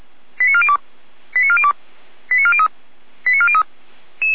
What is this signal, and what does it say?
Five-tone selective call (ZVEI), heard as quick runs of five tones, tones of about 80 ms. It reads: 89402 89402 89403 89403